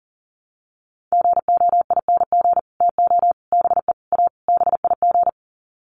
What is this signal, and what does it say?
Morse code, heard as a keyed tone, about 40 wpm, 700 Hz. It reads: GOING TO BE A BIG
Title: Sets of 5 Words Encore2 40wpm